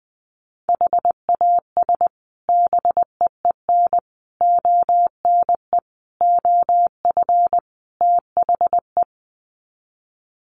Morse code, heard as a keyed tone, about 20 wpm, 700 Hz. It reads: HAS BEEN ONE OF THE
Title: Sets of 5 Words Encore 20wpm